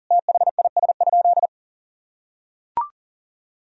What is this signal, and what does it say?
Morse code, heard as a keyed tone, about 40 wpm, 700 Hz. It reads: THIS?
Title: Sentences from Top 1000 Words - Review Part 2 - 40wpm